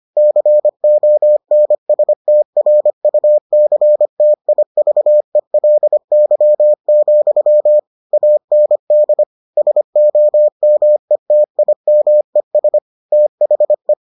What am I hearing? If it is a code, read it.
Morse code, heard as a keyed tone, about 25 wpm, 600 Hz. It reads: CONSTRUCTIVELY, AND SOMETIMES THE